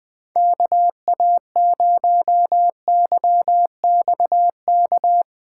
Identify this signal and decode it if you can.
Morse code, heard as a keyed tone, about 20 wpm, 700 Hz. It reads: KA0YXK
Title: USA Call Signs - 6 Characters - 6x Character Spacing - 20wpm